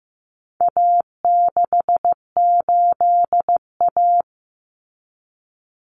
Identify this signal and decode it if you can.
Morse code, heard as a keyed tone, about 15 wpm, 700 Hz. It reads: A68A
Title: Common Cut Numbers - 4-Digit 15wpm